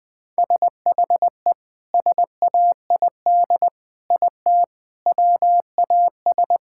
Morse code, heard as a keyed tone, about 20 wpm, 700 Hz. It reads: SHE SAID IT WAS
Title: Sentences with 2-Word Phrases - Spoken Beginning - 20wpm